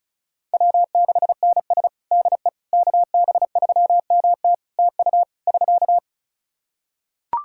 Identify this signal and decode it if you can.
Morse code, heard as a keyed tone, about 36 wpm, 700 Hz. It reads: W6NS DE KB3MT TU <SK>
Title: Practice QSO - Exchange 4 - Ending - 35wpm